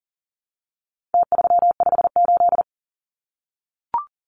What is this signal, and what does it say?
Morse code, heard as a keyed tone, about 40 wpm, 700 Hz. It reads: T358
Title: Numbers - Contest Style - 4-Digit 40wpm